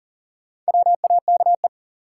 Morse code, heard as a keyed tone, about 37 wpm, 700 Hz. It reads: WAKE
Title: Sets of 4 Words - New Word Drill - 40wpm